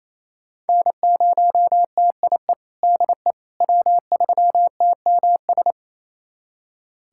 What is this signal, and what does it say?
Morse code, heard as a keyed tone, about 28 wpm, 700 Hz. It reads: N0TIE DE W3TMS
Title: USA Callsign DE Callsign - 5 Characters - 28wpm